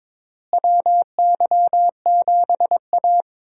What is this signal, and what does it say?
Morse code, heard as a keyed tone, about 22 wpm, 700 Hz. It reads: WY7A
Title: USA Call Signs - 4 Characters - Repeated 3x - 22wpm